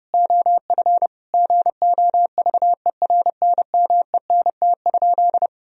Morse code, heard as a keyed tone, about 32 wpm, 700 Hz. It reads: OF GOVERNMENT?
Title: Sentences from Top 1000 Words - Review Part 3 - 30wpm